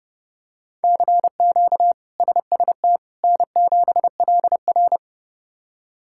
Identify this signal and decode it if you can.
Morse code, heard as a keyed tone, about 30 wpm, 700 Hz. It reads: CQ SST N7LR
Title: K1USN SST QSO Practice - 30wpm